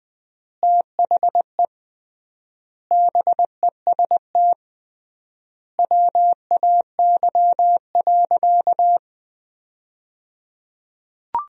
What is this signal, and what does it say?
Morse code, heard as a keyed tone, about 21 wpm, 700 Hz. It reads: THE BEST WAY.
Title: 3x Word-Spacing - Sentences from Top 400 Words - 20wpm